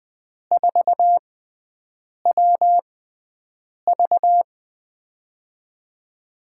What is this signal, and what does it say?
Morse code, heard as a keyed tone, about 20 wpm, spaced as for 7 wpm, 700 Hz. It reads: 4WV
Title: USA Call Signs - 5 Characters - 6x Character Spacing - 20wpm